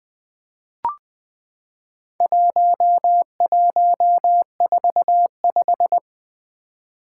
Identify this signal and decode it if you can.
Morse code, heard as a keyed tone, about 20 wpm, 700 Hz. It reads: 1145
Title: Four Challenging Letter-Number Combinations 20wpm